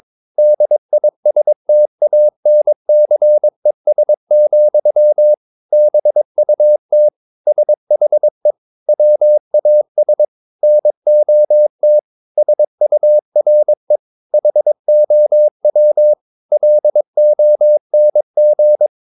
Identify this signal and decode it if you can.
Morse code, heard as a keyed tone, about 22 wpm, 600 Hz. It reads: DISTANCES, BUT SHE WAS NOT SURE HOW LONG
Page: Romanen Never af Ken Follet på engelsk.